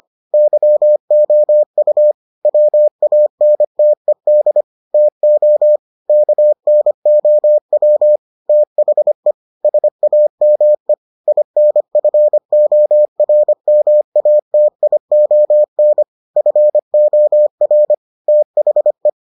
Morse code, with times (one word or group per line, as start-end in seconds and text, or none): 0.3-2.1 YOU
2.5-4.6 WANTED
4.9-5.8 TO
6.1-8.2 KNOW
8.5-9.3 THE
9.6-10.9 SAME
11.3-16.0 INFORMATION
16.4-17.9 FOR
18.3-19.1 THE